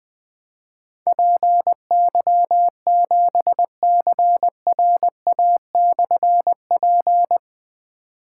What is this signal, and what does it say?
Morse code, heard as a keyed tone, about 20 wpm, 700 Hz. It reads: PY7CRA/P
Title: Difficult Call Signs 20wpm